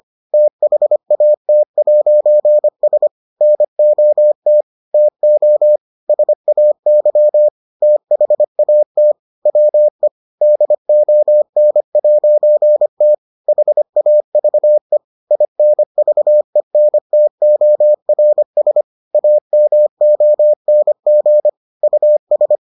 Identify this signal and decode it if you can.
Morse code, heard as a keyed tone, about 25 wpm, 600 Hz. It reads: THAT'S NOT TO SAY THAT WE DON'T HAVE INVENTORS AMONG US